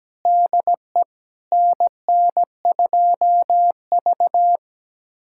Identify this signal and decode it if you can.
Morse code, heard as a keyed tone, about 17 wpm, 700 Hz. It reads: DE NN2V
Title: USA Callsign DE Callsign - 4 Characters - 17wpm